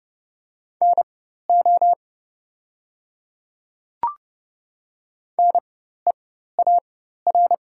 Morse code, heard as a keyed tone, about 30 wpm, spaced as for 14 wpm, 700 Hz. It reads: NO NEAR
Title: 4x Character Spacing - Top 300 Words - 30wpm